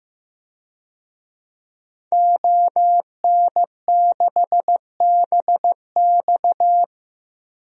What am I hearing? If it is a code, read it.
Morse code, heard as a keyed tone, about 15 wpm, 700 Hz. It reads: ON6BX
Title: DXCC-100 Call Signs - 5 Characters - 15wpm